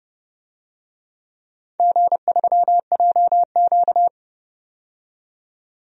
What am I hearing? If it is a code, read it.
Morse code, heard as a keyed tone, about 29 wpm, 700 Hz. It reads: G3JQ
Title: DXCC-100 Call Signs - 4 Characters - 30wpm